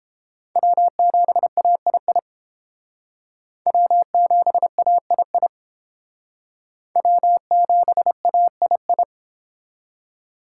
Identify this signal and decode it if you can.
Morse code, heard as a keyed tone, about 30 wpm, 700 Hz. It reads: W7AII W7AII W7AII